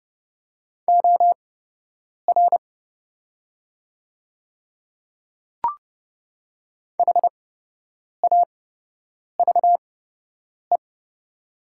Morse code, heard as a keyed tone, about 30 wpm, spaced as for 8 wpm, 700 Hz. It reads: OR HAVE
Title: 8x Character Spacing - Top 100 Words - 30wpm